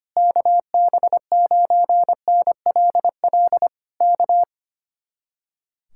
Morse code, heard as a keyed tone, about 24 wpm, 700 Hz. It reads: KB9NLL K
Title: Speed-Racing Common QSO Phrases - 25wpm